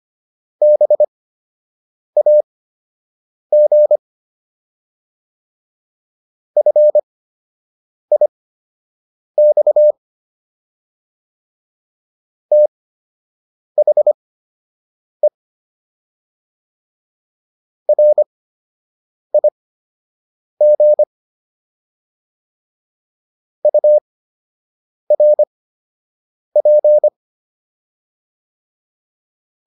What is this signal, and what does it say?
Morse code, heard as a keyed tone, about 25 wpm, spaced as for 7 wpm, 600 Hz. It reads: BAG FIX THE RIG URP